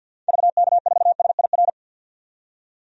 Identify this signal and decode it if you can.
Morse code, heard as a keyed tone, about 52 wpm, 700 Hz. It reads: VK4SIR
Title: DXCC-100 Call Signs - 6 Characters - 50wpm